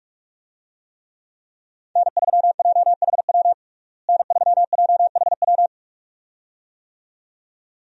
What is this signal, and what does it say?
Morse code, heard as a keyed tone, about 46 wpm, 700 Hz. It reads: N3JHW N3JHW
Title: USA Call Signs - 5 Characters - Repeated 2x - 45wpm